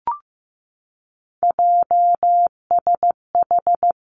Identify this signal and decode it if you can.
Morse code, heard as a keyed tone, about 15 wpm, 700 Hz. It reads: JSH
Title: Three Challenging Letter-Number Combinations 15wpm